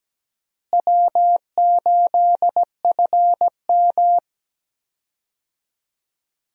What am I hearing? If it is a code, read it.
Morse code, heard as a keyed tone, about 17 wpm, 700 Hz. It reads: W8FM
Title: USA Call Signs - 4 Characters - Repeated 2x - 17wpm